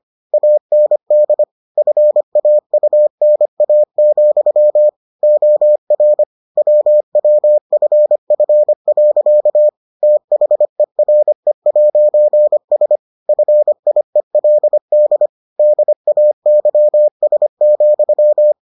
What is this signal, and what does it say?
Morse code, heard as a keyed tone, about 25 wpm, 600 Hz. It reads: AND FAUNA, OR WWFF. THERE'S FIELD DAYS,